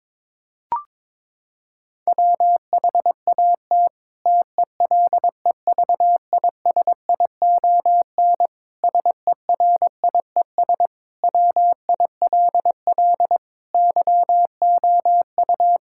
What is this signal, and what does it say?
Morse code, heard as a keyed tone, about 22 wpm, 700 Hz. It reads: WHAT TELEVISION SERIES WILL YOU
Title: Sentences from Top 1000 Words - Review Part 5 - 22wpm